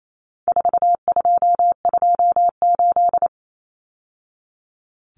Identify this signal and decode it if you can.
Morse code, heard as a keyed tone, about 28 wpm, 700 Hz. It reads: 4228
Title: Numbers - 4-Digit 28wpm